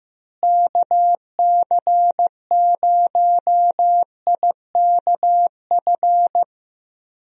Morse code, heard as a keyed tone, about 15 wpm, 700 Hz. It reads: KC0IKF
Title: USA Call Signs - 6 Characters - Repeated 3x - 15wpm